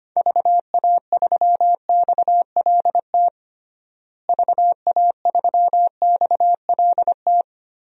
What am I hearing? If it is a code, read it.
Morse code, heard as a keyed tone, about 25 wpm, 700 Hz. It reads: VA3XLT VA3XLT
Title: DXCC-100 Call Signs - 6 Characters - Repeated 3x - 25wpm